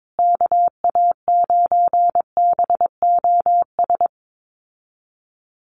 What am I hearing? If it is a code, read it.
Morse code, heard as a keyed tone, about 22 wpm, 700 Hz. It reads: KA9BOS
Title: USA Call Signs Repeated 3x 22wpm